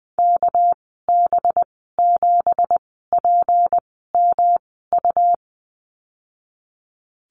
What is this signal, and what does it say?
Morse code, heard as a keyed tone, about 19 wpm, spaced as for 14 wpm, 700 Hz. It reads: KB7PMU
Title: 200 Lesson 32g - Letter k - Call Signs 20wpm